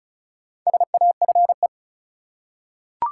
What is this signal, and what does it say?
Morse code, heard as a keyed tone, about 36 wpm, 700 Hz. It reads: SAFE
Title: Four Letter Words - 35wpm